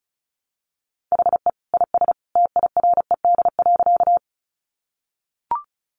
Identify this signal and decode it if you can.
Morse code, heard as a keyed tone, about 34 wpm, 700 Hz. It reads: HE IS TIRED.
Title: Additional Easy Phrases 35wpm